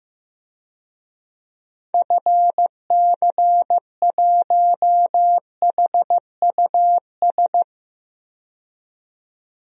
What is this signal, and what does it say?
Morse code, heard as a keyed tone, about 15 wpm, 700 Hz. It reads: FC1HUS
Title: DXCC-100 Call Signs - 6 Characters - 15wpm